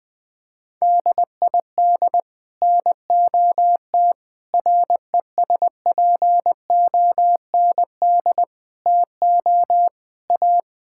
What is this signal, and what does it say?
Morse code, heard as a keyed tone, about 20 wpm, 700 Hz. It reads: DID NOT RESPOND TO A
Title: Sets of 8 Words - 20wpm